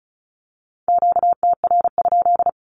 Morse code, heard as a keyed tone, about 34 wpm, 700 Hz. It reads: QTR?
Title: Q-Codes 35wpm